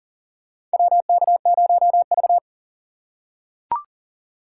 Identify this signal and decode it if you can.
Morse code, heard as a keyed tone, about 40 wpm, 700 Hz. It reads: WK0V
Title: USA Call Signs - 4 Characters - 2x Character Spacing - 40wpm